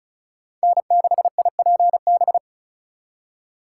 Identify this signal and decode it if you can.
Morse code, heard as a keyed tone, about 45 wpm, 700 Hz. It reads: N6IPB